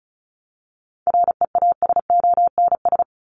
Morse code, heard as a keyed tone, about 34 wpm, 700 Hz. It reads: REASONS
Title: ES:S Suffix - Top 100 Words with Base Words 35wpm